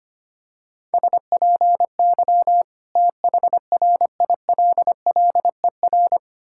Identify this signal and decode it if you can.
Morse code, heard as a keyed tone, about 25 wpm, 700 Hz. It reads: SPY THRILLER
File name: practice / 25wpm / Difficult Words with Hints - 25wpm